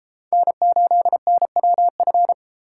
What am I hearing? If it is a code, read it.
Morse code, heard as a keyed tone, about 33 wpm, 700 Hz. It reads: N8NWF